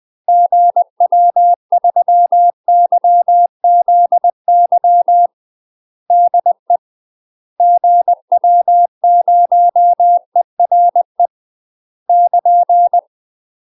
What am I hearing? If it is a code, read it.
Morse code, heard as a keyed tone, about 20 wpm, 700 Hz. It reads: GW3YZY DE GW0ERE <KN>
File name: UK MASTER QSOs TRAINING / 20WPM / UK MASTER QSOs 0008